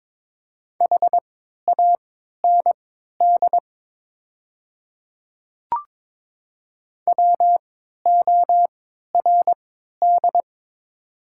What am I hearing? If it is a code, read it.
Morse code, heard as a keyed tone, about 22 wpm, spaced as for 13 wpm, 700 Hz. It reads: HAND WORD
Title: 3x Character Spacing - Top 200 Words - 22wpm